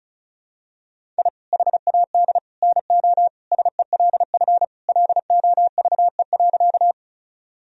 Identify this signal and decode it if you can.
Morse code, heard as a keyed tone, about 36 wpm, 700 Hz. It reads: I HAD NO SELF LOVE.
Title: Sentences from Top 1000 Words - Review Part 1 - 35wpm